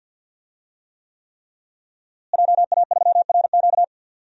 Morse code, heard as a keyed tone, about 49 wpm, 700 Hz. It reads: JA3RQ
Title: DXCC-100 Call Signs - 5 Characters - Repeated 3x - 50wpm